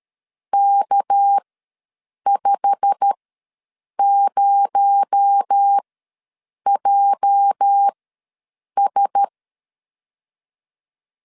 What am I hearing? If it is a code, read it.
Morse code, heard as a keyed tone, about 13 wpm, spaced as for 7 wpm, 800 Hz. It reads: K50JS
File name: LEZIONI / 8 YJ45 / 8 YJ45 ws 12wpm 3146